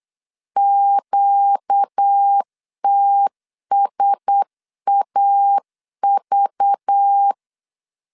Morse code, heard as a keyed tone, about 8 wpm, 800 Hz. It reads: QTSAV
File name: LEZIONI / 8 YJ45 / 8 Y 8wpm 3245